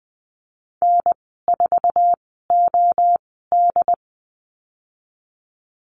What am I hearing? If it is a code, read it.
Morse code, heard as a keyed tone, about 20 wpm, spaced as for 15 wpm, 700 Hz. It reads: N4OD
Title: 088 Lesson 15g - Number 5 - Call Signs 20wpm